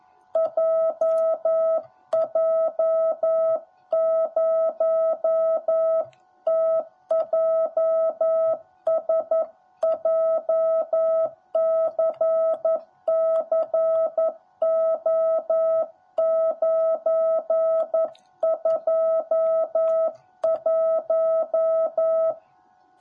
Morse code, heard as a keyed tone, about 11 wpm, 650 Hz. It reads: JJ0TJS JCCO921
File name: jh8cbh / 2021cw / cw10b